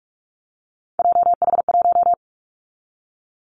Morse code, heard as a keyed tone, about 44 wpm, 700 Hz. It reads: JH1
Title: Callsign Prefixes for Top 25 DXCC - 45wpm